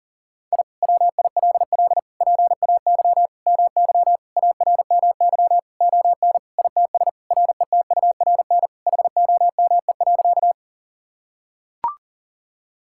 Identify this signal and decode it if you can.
Morse code, heard as a keyed tone, about 40 wpm, 700 Hz. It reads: I WILL PAY MY ARMY ON ITS RETURN HOME.
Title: Sentences from Top 1000 Words - Review Part 5 - 40wpm